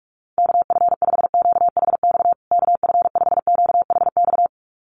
Morse code, heard as a keyed tone, about 44 wpm, 700 Hz. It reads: KF5QHX KF5QHX
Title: USA Call Signs Repeated 2x 45wpm